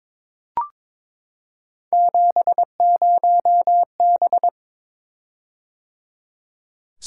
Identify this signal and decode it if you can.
Morse code, heard as a keyed tone, about 22 wpm, 700 Hz. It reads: 70B
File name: practice / 22wpm / Three Letter-Number Combinations - 22wpm